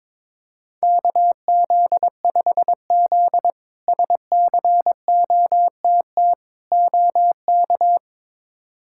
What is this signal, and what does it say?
Morse code, heard as a keyed tone, about 21 wpm, 700 Hz. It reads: KZ5Z SCOTT OK
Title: K1USN SST QSO Practice - 22wpm